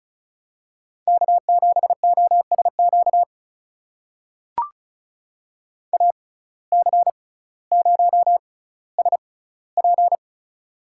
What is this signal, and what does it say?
Morse code, heard as a keyed tone, about 34 wpm, spaced as for 18 wpm, 700 Hz. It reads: K7OSQ AC0SP
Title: USA Call Signs - 5 Characters - 6x Character Spacing - 35wpm